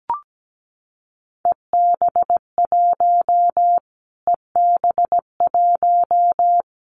Morse code, heard as a keyed tone, about 17 wpm, 700 Hz. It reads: EB1 EB1
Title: Callsign Prefixes for Top 25 DXCC - Repeated 3x - 17wpm